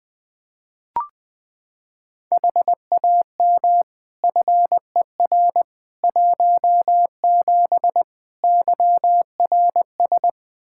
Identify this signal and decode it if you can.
Morse code, heard as a keyed tone, about 20 wpm, 700 Hz. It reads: HAM FER 17 YRS
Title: Practice QSO - Exchange 3 - More Info - 20wpm